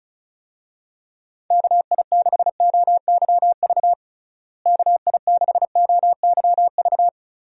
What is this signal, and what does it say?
Morse code, heard as a keyed tone, about 36 wpm, 700 Hz. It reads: KI6OYV KI6OYV
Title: USA Call Signs - 6 Characters - Repeated 2x - 35wpm